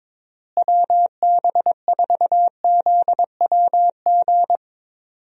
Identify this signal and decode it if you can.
Morse code, heard as a keyed tone, about 22 wpm, 700 Hz. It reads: WB4ZWG
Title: USA Call Signs - 6 Characters - Repeated 3x - 22wpm